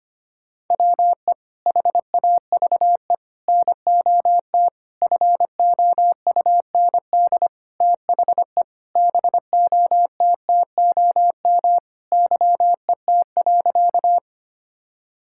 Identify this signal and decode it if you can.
Morse code, heard as a keyed tone, about 25 wpm, 700 Hz. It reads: WE HAVE NOT FOUND THE BOTTOM YET.
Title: Sentences from Top 1000 Words - Review Part 6 - 25wpm